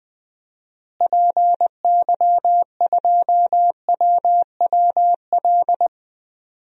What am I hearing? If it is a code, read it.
Morse code, heard as a keyed tone, about 20 wpm, 700 Hz. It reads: PY2WWL
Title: DXCC-100 Call Signs - 6 Characters - Repeated 2x - 20wpm